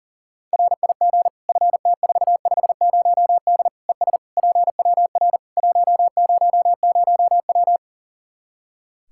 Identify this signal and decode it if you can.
Morse code, heard as a keyed tone, about 40 wpm, 700 Hz. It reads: RIG FT450D ES PWR 100W
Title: K1USN SST QSO Practice 40wpm